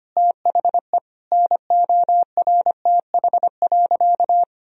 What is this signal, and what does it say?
Morse code, heard as a keyed tone, about 25 wpm, 700 Hz. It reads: THE NORTH.
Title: Sentences from Top 1000 Words - Review Part 2 - 25wpm